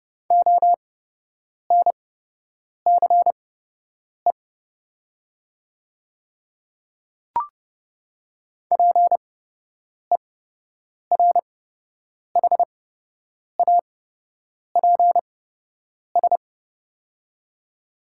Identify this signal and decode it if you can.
Morse code, heard as a keyed tone, about 30 wpm, spaced as for 8 wpm, 700 Hz. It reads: ONCE PERHAPS
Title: 8x Character Spacing - Top 500 Words - 30wpm